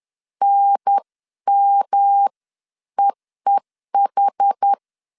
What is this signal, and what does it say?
Morse code, heard as a keyed tone, about 10 wpm, spaced as for 9 wpm, 800 Hz. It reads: NMEEH
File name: 10wpm